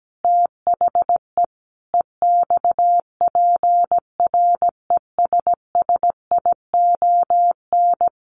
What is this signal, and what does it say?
Morse code, heard as a keyed tone, about 17 wpm, 700 Hz. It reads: THE EXPRESSION
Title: Sentences from Top 1400 Words 17wpm